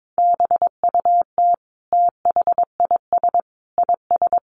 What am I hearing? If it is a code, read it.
Morse code, heard as a keyed tone, about 22 wpm, 700 Hz. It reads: BUT THIS IS
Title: Sentences from Top 1500 Words 22wpm